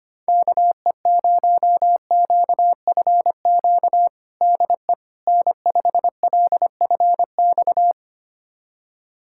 Morse code, heard as a keyed tone, about 25 wpm, 700 Hz. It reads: KE0QFQ DE N5LFX
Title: USA Callsign DE Callsign 25wpm